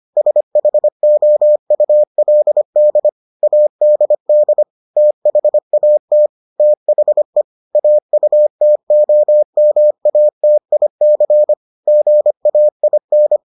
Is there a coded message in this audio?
Yes: Morse code from 0.2 to 13.4 s, heard as a keyed tone, about 25 wpm, 600 Hz. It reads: SHOULD ADD THAT THE AUTOMATIC GAIN